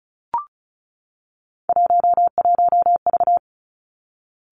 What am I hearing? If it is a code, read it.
Morse code, heard as a keyed tone, about 35 wpm, 700 Hz. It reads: 11V
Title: Three Challenging Letter-Number Combinations 35wpm